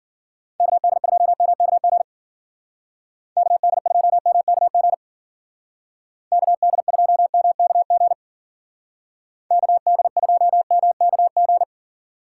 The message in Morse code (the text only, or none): KD2MKG KD2MKG KD2MKG KD2MKG